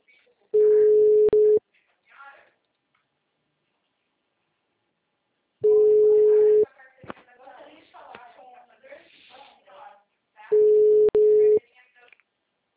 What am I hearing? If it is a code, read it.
Ringback tone (Europe, 425 Hz), with speech.